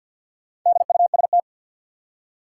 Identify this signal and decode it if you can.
Morse code, heard as a keyed tone, about 50 wpm, 700 Hz. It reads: DUST
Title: Four Letter Words - 50wpm